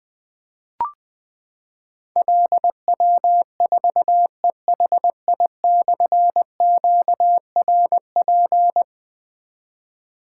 Morse code, heard as a keyed tone, about 20 wpm, 700 Hz. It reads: LW4EHI/QRP
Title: Difficult Call Signs 20wpm